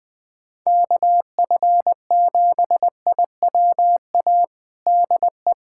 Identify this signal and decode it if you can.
Morse code, heard as a keyed tone, about 20 wpm, 700 Hz. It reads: KF7IWA DE
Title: USA Callsign DE Callsign 20wpm